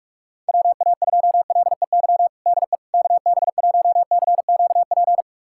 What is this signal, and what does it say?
Morse code, heard as a keyed tone, about 41 wpm, 690 Hz. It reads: WA2LEY DE KB1CQP
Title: USA Callsign DE Callsign 45wpm